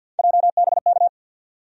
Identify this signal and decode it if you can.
Morse code, heard as a keyed tone, about 53 wpm, 700 Hz. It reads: JBK
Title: USA Call Sign Postfixes - 3 Letters 50wpm